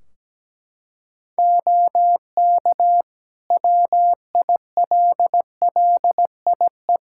Morse code, heard as a keyed tone, about 17 wpm, 700 Hz. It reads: OK WILLIE
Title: Practice QSO - Exchange 3 - More Info - 17wpm